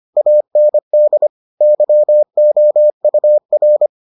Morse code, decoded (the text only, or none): AND YOUR